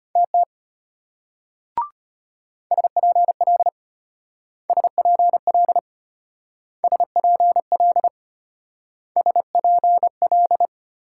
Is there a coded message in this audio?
Yes: Morse code from 0.2 to 10.7 s, heard as a keyed tone, about 32 wpm, 700 Hz. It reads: TT SPL SPL SPL SPL